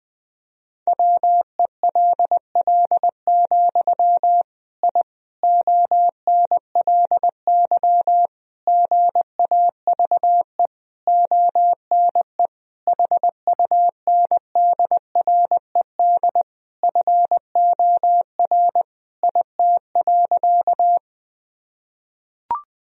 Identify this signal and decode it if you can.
Morse code, heard as a keyed tone, about 20 wpm, 700 Hz. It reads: WELL, I ONLY GAVE ONE HUNDRED FOR IT.
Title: Sentences from Top 1000 Words - Review Part 2 - 20wpm